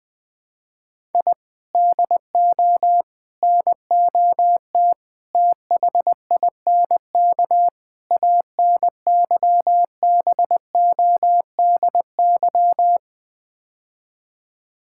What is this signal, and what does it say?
Morse code, heard as a keyed tone, about 21 wpm, 690 Hz. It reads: I DO NOT THINK ANYBODY
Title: Build A Phrase - 5-Word Phrases - 20wpm